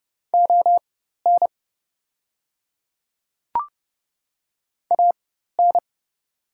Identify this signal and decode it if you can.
Morse code, heard as a keyed tone, about 30 wpm, spaced as for 14 wpm, 690 Hz. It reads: ON AN
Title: 4x Character Spacing - Top 100 Words - 30wpm